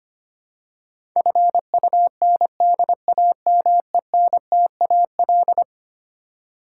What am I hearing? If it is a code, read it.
Morse code, heard as a keyed tone, about 25 wpm, 700 Hz. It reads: FUNDAMENTAL
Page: morse-code-ninja-practice-25wpm